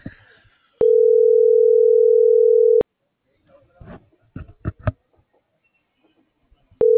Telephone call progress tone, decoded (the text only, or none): ringback tone